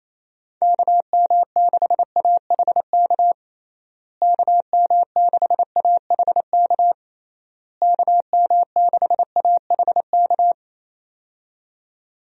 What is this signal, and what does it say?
Morse code, heard as a keyed tone, about 28 wpm, 700 Hz. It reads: KM6AHK KM6AHK KM6AHK
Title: USA Call Signs - 6 Characters - Repeated 3x - 28wpm